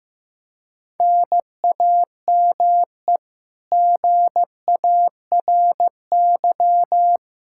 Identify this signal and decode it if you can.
Morse code, heard as a keyed tone, about 15 wpm, 700 Hz. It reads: NAME GARY
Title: Practice QSO - Exchange 1 - Beginning - 15wpm